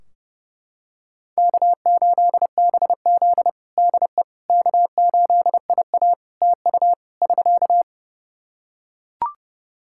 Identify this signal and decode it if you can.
Morse code, heard as a keyed tone, about 30 wpm, 700 Hz. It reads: K8BZ DE K8IA TU <SK>
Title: K1USN SST QSO Practice 30wpm